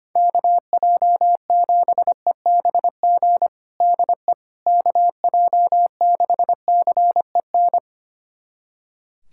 Morse code, heard as a keyed tone, about 25 wpm, 700 Hz. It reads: KJ7EBG DE KJ6CEN
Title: USA Callsign DE Callsign - 6 Characters - 25wpm